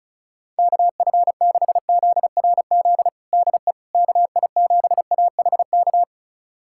Morse code, heard as a keyed tone, about 36 wpm, 700 Hz. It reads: KF6ZRZ DE KI7AHK
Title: USA Callsign DE Callsign 35wpm